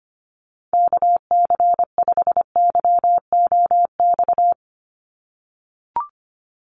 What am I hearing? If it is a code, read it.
Morse code, heard as a keyed tone, about 25 wpm, 700 Hz. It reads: KC5YOX
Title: USA Call Signs 25wpm